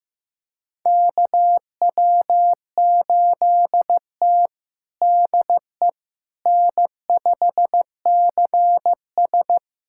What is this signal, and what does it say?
Morse code, heard as a keyed tone, about 15 wpm, 700 Hz. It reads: KW8T DE N5CS
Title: USA Callsign DE Callsign - 4 Characters - 15wpm